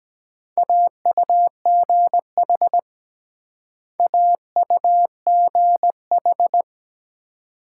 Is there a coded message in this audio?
Yes: Morse code from 0.6 to 6.6 s, heard as a keyed tone, about 22 wpm, 700 Hz. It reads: AUGH AUGH